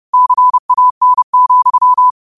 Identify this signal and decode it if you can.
Morse code, heard as a keyed tone, about 30 wpm, 1000 Hz. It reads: CAN,
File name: qotdfiles / 30wpm